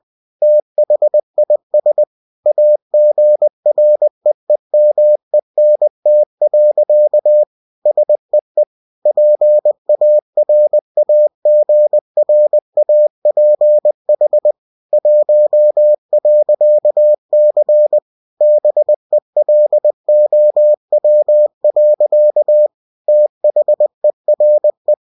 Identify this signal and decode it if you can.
Morse code, heard as a keyed tone, about 20 wpm, 600 Hz. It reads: THIS AGREEMENT. SEE PARAGRAPH 1.C BELOW. THERE
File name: cw-books / winnie-the-pooh / winnie-20wpm-page-0113